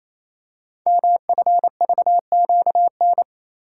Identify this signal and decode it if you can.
Morse code, heard as a keyed tone, about 29 wpm, 700 Hz. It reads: MFVQN
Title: Code Groups - 5 Letters - 28wpm